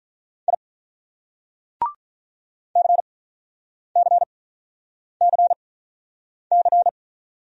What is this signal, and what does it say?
Morse code, heard as a keyed tone, about 47 wpm, spaced as for 9 wpm, 700 Hz. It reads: I CCCC